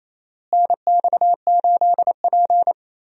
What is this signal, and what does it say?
Morse code, heard as a keyed tone, about 28 wpm, 700 Hz. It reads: NX8P
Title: USA Call Signs - 4 Characters - Repeated 2x - 28wpm